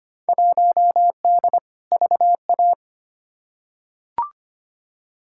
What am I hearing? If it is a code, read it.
Morse code, heard as a keyed tone, about 25 wpm, 700 Hz. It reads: 1D VA
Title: Field Day Exchanges - 25wpm